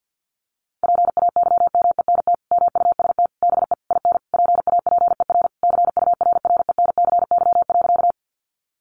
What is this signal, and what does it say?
Morse code, heard as a keyed tone, about 50 wpm, 700 Hz. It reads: PAYMENT MUST BE IN PAPER CURRENCY.
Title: Sentences from Top 1500 Words 50wpm